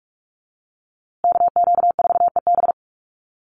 Morse code, heard as a keyed tone, about 44 wpm, 690 Hz. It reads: KQ4EB
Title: USA Call Signs Repeated 3x 45wpm